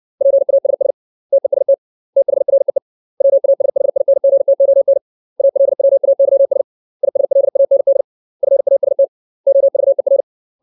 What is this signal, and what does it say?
Morse code, heard as a keyed tone, about 60 wpm, 550 Hz. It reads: PASS TEST THREE WASHINGTON ALWAYS ISLAND FAST OVER
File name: 632 words / 632 Words 60WPM